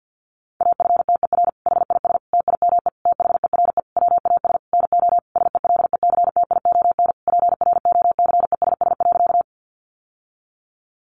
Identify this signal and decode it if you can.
Morse code, heard as a keyed tone, about 50 wpm, 700 Hz. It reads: AFTER HIS TIME THERE WAS NO SELECTION PROCESS.
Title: Sentences from Top 1400 Words 50wpm